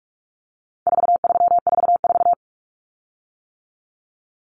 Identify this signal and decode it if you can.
Morse code, heard as a keyed tone, about 44 wpm, 700 Hz. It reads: 4344
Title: Common Cut Numbers - 4-Digit 45wpm